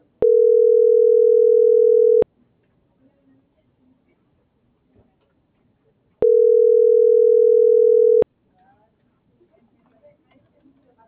Ringback tone (North America), 440 + 480 Hz.
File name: CENTRAL TERRESTRE / out karis - .76987